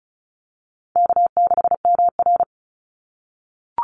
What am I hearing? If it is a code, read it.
Morse code, heard as a keyed tone, about 34 wpm, 700 Hz. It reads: K6MR
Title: Call Signs - 35wpm